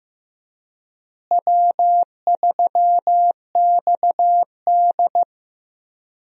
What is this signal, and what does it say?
Morse code, heard as a keyed tone, about 15 wpm, 700 Hz. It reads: W3XD
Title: USA Call Signs - 4 Characters - Repeated 3x - 15wpm